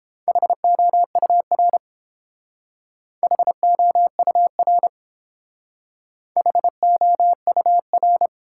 Morse code, heard as a keyed tone, about 30 wpm, 700 Hz. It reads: HOUR HOUR HOUR